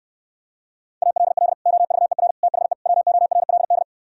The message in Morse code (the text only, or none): ALL OVER THE WORLD